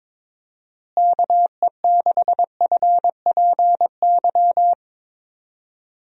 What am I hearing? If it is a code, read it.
Morse code, heard as a keyed tone, about 21 wpm, 700 Hz. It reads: KE6FPY
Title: USA Call Signs - 6 Characters - Repeated 3x - 22wpm